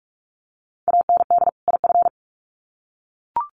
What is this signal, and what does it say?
Morse code, heard as a keyed tone, about 44 wpm, 700 Hz. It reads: AND IF
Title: Sets of 2 Words 45wpm